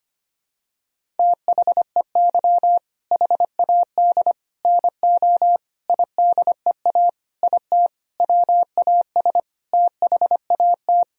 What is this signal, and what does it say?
Morse code, heard as a keyed tone, about 25 wpm, 700 Hz. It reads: THEY HAD NO IDEA IT WAS THAT
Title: Sentences from Top 1000 Words - Review Part 1 - 25wpm